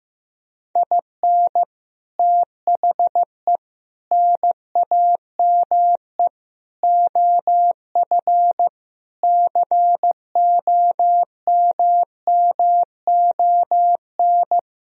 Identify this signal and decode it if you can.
Morse code, heard as a keyed tone, about 16 wpm, 700 Hz. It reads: IN THE NAME OF COMMON
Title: Sentences from Top 1000 Words - Review Part 4 - 15wpm